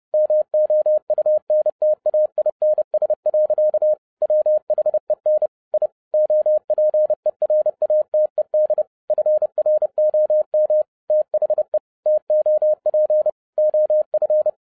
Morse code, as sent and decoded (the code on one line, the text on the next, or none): -- --- ..- -. - .- .. -. ... .-.-.- / .-- .... -. / .. / --- .--. .-. .- - . -.. / ..-. .-. --- -- / - .... / - --- .--. / --- ..-.
MOUNTAINS. WHEN I OPERATED FROM THE TOP OF